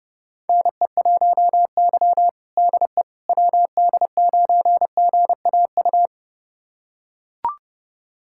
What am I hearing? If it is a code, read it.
Morse code, heard as a keyed tone, about 32 wpm, 700 Hz. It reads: NE1Y DE WD9GAU
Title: USA Callsign DE Callsign 30wpm